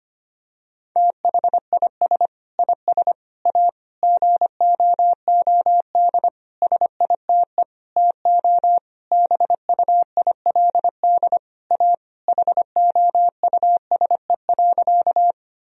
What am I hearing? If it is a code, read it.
Morse code, heard as a keyed tone, about 25 wpm, 700 Hz. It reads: THIS IS A GOOD SITE TO BUILD A HOUSE.
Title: Sentences from Top 1000 Words - Review Part 4 - 25wpm